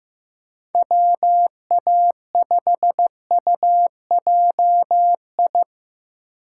Morse code, heard as a keyed tone, about 15 wpm, 700 Hz. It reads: WA5UJI
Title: USA Call Signs - 3x Character Spacing - 15wpm